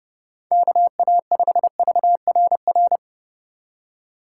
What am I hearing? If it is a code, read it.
Morse code, heard as a keyed tone, about 30 wpm, 700 Hz. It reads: KA5VRR
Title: USA Call Signs - 6 Characters - 2x Character Spacing - 30wpm